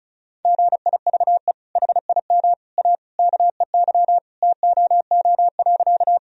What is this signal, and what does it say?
Morse code, heard as a keyed tone, about 34 wpm, 700 Hz. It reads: GIVE HIM A KEY TOO.
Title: Sentences from Top 1000 Words - Review Part 4 - 35wpm